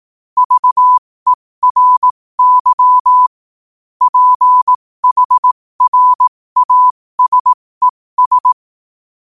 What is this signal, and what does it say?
Morse code, heard as a keyed tone, about 18 wpm, spaced as for 15 wpm, 1000 Hz. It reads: VERY PHRASES
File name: qotdfiles / 15wpm